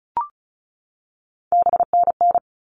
Morse code, heard as a keyed tone, about 34 wpm, 700 Hz. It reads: BNN
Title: All Cut Numbers - 3-Digit 35wpm